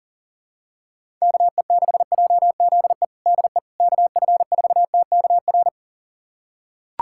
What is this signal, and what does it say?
Morse code, heard as a keyed tone, about 40 wpm, 690 Hz. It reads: KE6JZE DE KF4TKR
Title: USA Callsign DE Callsign - 6 Characters - 40wpm